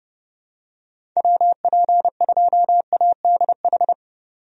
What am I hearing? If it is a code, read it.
Morse code, heard as a keyed tone, about 32 wpm, 700 Hz. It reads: WP2ADH
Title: USA Call Signs - 6 Characters - 30wpm